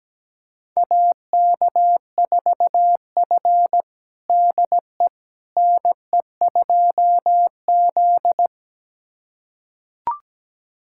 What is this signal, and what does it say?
Morse code, heard as a keyed tone, about 17 wpm, 700 Hz. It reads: AK4F DE NE2Z
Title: USA Callsign DE Callsign - 4 Characters - 17wpm